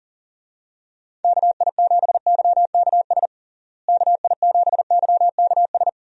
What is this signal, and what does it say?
Morse code, heard as a keyed tone, about 40 wpm, 700 Hz. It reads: KI7YKS KI7YKS
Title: USA Call Signs - 6 Characters - Repeated 3x - 40wpm